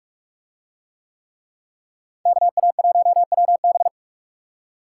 Morse code, heard as a keyed tone, about 38 wpm, 700 Hz. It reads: KA1WB